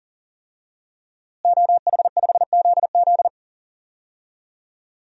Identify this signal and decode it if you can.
Morse code, heard as a keyed tone, about 40 wpm, 700 Hz. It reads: OH5ZZ
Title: DXCC-100 Call Signs - 5 Characters - Repeated 3x - 40wpm